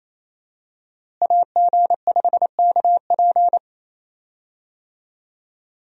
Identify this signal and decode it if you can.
Morse code, heard as a keyed tone, about 28 wpm, 700 Hz. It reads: AG5KP
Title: USA Call Signs - 5 Characters - Repeated 2x - 28wpm